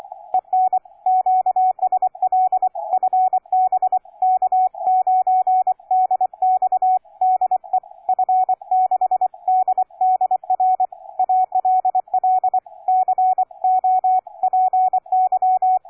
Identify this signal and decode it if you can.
Morse code, heard as a keyed tone, about 24 wpm, 740 Hz, with radio noise. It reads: EN QSL FB K9DX DE F6DDR ALL COPY
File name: cw forum / K9DX 1818 le 17-12-2008 05h47